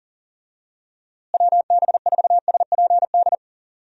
Morse code, heard as a keyed tone, about 37 wpm, 700 Hz. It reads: WB4SPD
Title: USA Call Signs - 6 Characters - Repeated 2x - 40wpm